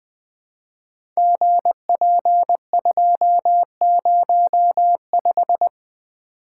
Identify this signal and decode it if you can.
Morse code, heard as a keyed tone, about 20 wpm, 700 Hz. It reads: GP205
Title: Code Groups - 5 Letter:Numbers - 20wpm